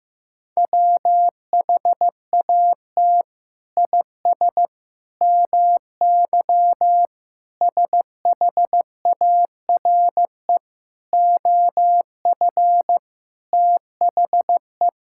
Morse code, heard as a keyed tone, about 15 wpm, 700 Hz. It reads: WHAT IS MY SHARE OF THE
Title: Sentences from Top 1000 Words - Review Part 6 - 15wpm